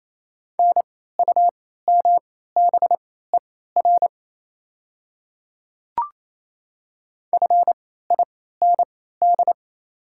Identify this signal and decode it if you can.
Morse code, heard as a keyed tone, about 28 wpm, spaced as for 16 wpm, 700 Hz. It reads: NUMBER FIND
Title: 3x Character Spacing - Top 100 Words - 28wpm